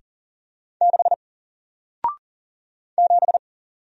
Morse code, heard as a keyed tone, about 40 wpm, 700 Hz. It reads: 6 7
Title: Single Letter-Number Alt 40wpm